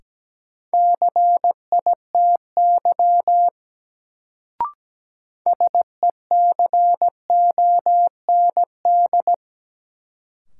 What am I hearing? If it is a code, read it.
Morse code, heard as a keyed tone, about 17 wpm, 700 Hz. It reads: CITY SECOND
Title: Sentences from Top 300 Words - New Word Drill - 17wpm